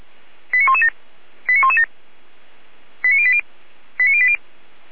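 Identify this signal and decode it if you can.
Five-tone selective call (ZVEI), heard as quick runs of five tones, tones of about 80 ms. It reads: 89117 89117 89080 89080
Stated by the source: ZVEI five-tone selective call received on SDR.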